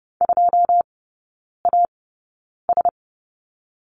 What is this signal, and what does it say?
Morse code, heard as a keyed tone, about 30 wpm, spaced as for 9 wpm, 700 Hz. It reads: 2AS
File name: course / 30:10wpm / 087 Lesson 15e - Number 5 - Three Characters 30wpm